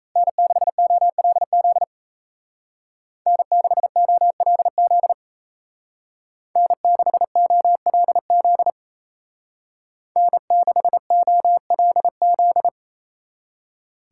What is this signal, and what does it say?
Morse code, heard as a keyed tone, about 37 wpm, 700 Hz. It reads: N6OLZ N6OLZ N6OLZ N6OLZ